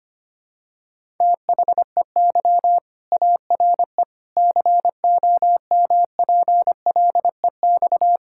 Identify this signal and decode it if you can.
Morse code, heard as a keyed tone, about 25 wpm, 700 Hz. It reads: THEY ARE COMPLEX
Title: Sentences from Top 1000 Words - Review Part 6 - 25wpm